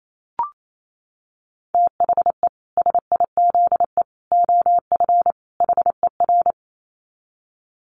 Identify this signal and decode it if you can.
Morse code, heard as a keyed tone, about 28 wpm, 700 Hz. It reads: THE SIZE OF HER
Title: Dit Challenge 28wpm